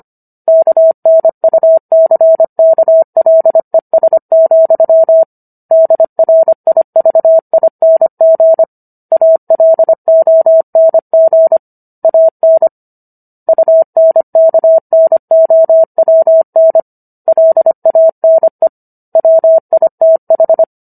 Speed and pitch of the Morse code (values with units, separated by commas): 25 wpm, 650 Hz